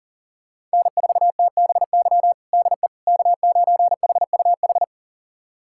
Morse code, heard as a keyed tone, about 37 wpm, 700 Hz. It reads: N4TBY DE K9HUH
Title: USA Callsign DE Callsign - 5 Characters - 40wpm